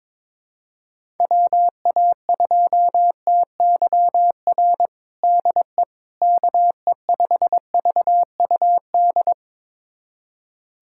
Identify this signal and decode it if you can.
Morse code, heard as a keyed tone, about 22 wpm, 700 Hz. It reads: WA2TYR DE KE5VUD
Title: USA Callsign DE Callsign - 6 Characters - 22wpm